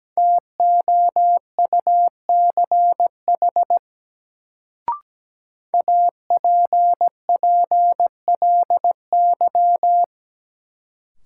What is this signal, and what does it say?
Morse code, heard as a keyed tone, about 17 wpm, 700 Hz. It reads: TOUCH APPLY
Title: Sentences from Top 1200 Words - New Word Drill Alt - 17wpm